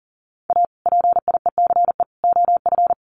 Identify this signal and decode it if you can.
Morse code, heard as a keyed tone, about 40 wpm, 700 Hz. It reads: A PIECE OF
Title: Sets of 3 Words 40wpm